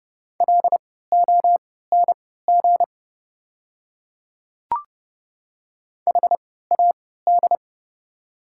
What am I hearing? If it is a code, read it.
Morse code, heard as a keyed tone, about 30 wpm, spaced as for 17 wpm, 690 Hz. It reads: LONG HAD
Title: 3x Character Spacing - Top 100 Words - 30wpm